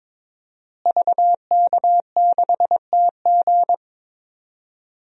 Morse code, heard as a keyed tone, about 22 wpm, 700 Hz. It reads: VK6TG
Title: DXCC-100 Call Signs - 5 Characters - Repeated 3x - 22wpm